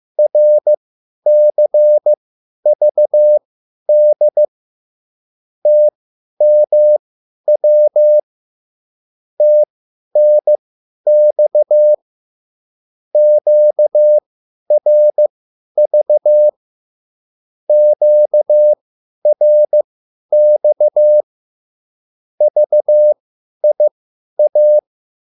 Morse code, heard as a keyed tone, about 15 wpm, spaced as for 10 wpm, 600 Hz. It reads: RCVD TMW TNX QRV QRX VIA
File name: QSL / CW lesson